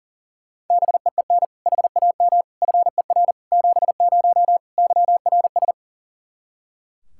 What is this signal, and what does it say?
Morse code, heard as a keyed tone, about 40 wpm, 700 Hz. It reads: BEEN HAM FER 70 YRS
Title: Practice QSO - Exchange 3 - More Info - 40wpm